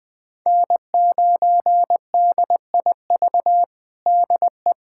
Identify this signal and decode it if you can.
Morse code, heard as a keyed tone, about 21 wpm, 700 Hz. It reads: N9DIV DE
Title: USA Callsign DE Callsign - 5 Characters - 20wpm